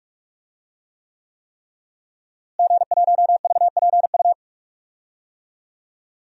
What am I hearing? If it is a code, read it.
Morse code, heard as a keyed tone, about 46 wpm, 700 Hz. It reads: G1VPU
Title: DXCC-100 Call Signs - 5 Characters - Repeated 3x - 45wpm